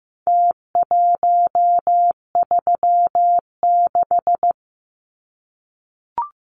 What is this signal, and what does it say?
Morse code, heard as a keyed tone, about 15 wpm, 700 Hz. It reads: T136
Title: Numbers - Contest Style - 4-Digit 15wpm